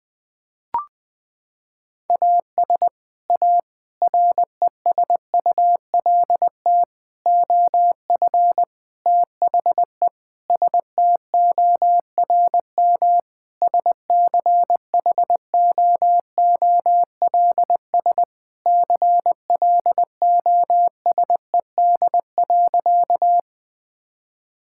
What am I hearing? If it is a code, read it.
Morse code, heard as a keyed tone, about 20 wpm, 700 Hz. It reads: AS A RESULT OF THE STORM SCHOOLS CLOSED.
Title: Sentences with 5-Word Phrases - 20wpm